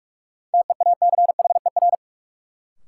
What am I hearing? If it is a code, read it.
Morse code, heard as a keyed tone, about 46 wpm, 700 Hz. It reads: TEACHER
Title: Sentences from Top 900 Words - New Word Drill - 45wpm